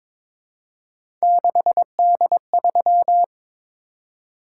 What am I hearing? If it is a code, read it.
Morse code, heard as a keyed tone, about 22 wpm, 700 Hz. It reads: 6D3
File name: practice / 22wpm / Three Letter-Number Combinations - 22wpm